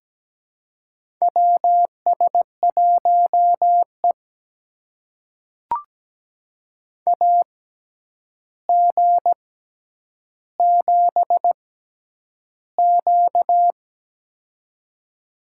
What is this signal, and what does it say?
Morse code, heard as a keyed tone, about 17 wpm, spaced as for 9 wpm, 700 Hz. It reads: WS1E AG7Q
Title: USA Call Signs - 4 Characters - 6x Character Spacing - 17wpm